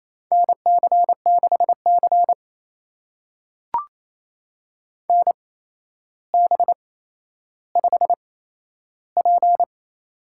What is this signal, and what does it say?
Morse code, heard as a keyed tone, about 28 wpm, spaced as for 12 wpm, 700 Hz. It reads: NC6C NB5P
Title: USA Call Signs - 4 Characters - 8x Character Spacing - 28wpm